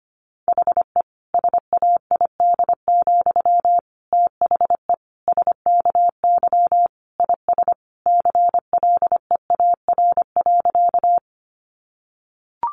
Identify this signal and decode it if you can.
Morse code, heard as a keyed tone, about 25 wpm, 700 Hz. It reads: HE SAID, THE SKY IS CLEAR.
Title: Sentences from Top 1200 Words 25wpm